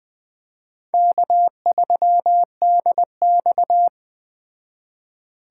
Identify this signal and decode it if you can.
Morse code, heard as a keyed tone, about 20 wpm, 700 Hz. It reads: K3DX